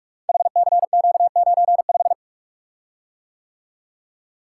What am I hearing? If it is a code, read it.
Morse code, heard as a keyed tone, about 44 wpm, 700 Hz. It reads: HCQ95
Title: Code Groups - 5 Letter:Numbers - 45wpm